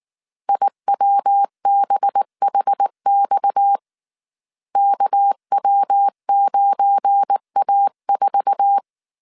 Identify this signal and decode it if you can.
Morse code, heard as a keyed tone, about 19 wpm, 800 Hz. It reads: IWBHX KW9A4
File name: CW / DET / 1 9 1500C S A 18wpm 2459